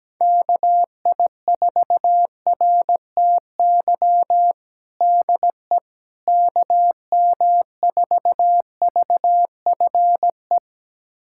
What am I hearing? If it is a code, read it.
Morse code, heard as a keyed tone, about 17 wpm, 700 Hz. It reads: KI4RTY DE KM4VFE
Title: USA Callsign DE Callsign - 6 Characters - 17wpm